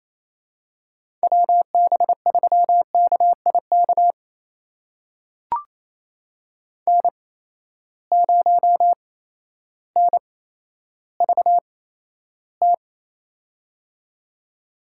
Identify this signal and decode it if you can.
Morse code, heard as a keyed tone, about 28 wpm, spaced as for 13 wpm, 700 Hz. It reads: WB3KIK N0NVT
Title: USA Call Signs - 8x Character Spacing - 28wpm